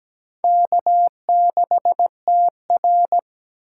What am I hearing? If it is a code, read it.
Morse code, heard as a keyed tone, about 17 wpm, 700 Hz. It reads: K6TR
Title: USA Call Signs - 4 Characters - 2x Character Spacing - 17wpm